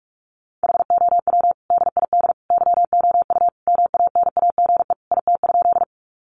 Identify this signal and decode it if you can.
Morse code, heard as a keyed tone, about 44 wpm, 700 Hz. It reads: HOW DID YOU MANAGE IT?
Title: Sentences from Top 1400 Words 45wpm